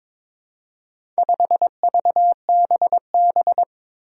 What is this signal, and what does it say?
Morse code, heard as a keyed tone, about 22 wpm, 700 Hz. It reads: 5VBB
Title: Four Challenging Letter-Number Combinations 22wpm